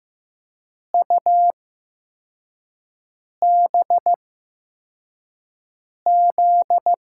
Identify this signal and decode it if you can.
Morse code, heard as a keyed tone, about 15 wpm, spaced as for 4 wpm, 700 Hz. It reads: UBZ
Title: USA Call Signs - 6 Characters - 8x Character Spacing - 15wpm